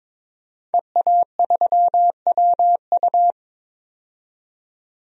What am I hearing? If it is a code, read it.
Morse code, heard as a keyed tone, about 22 wpm, 700 Hz. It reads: EA3WU
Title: DXCC-100 Call Signs - 5 Characters - 22wpm